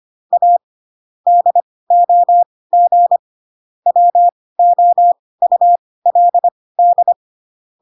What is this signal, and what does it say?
Morse code, heard as a keyed tone, about 25 wpm, spaced as for 18 wpm, 700 Hz. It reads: A DOG WOULD
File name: Pinocchio / Pinocchio ch5 12 25 ik1hge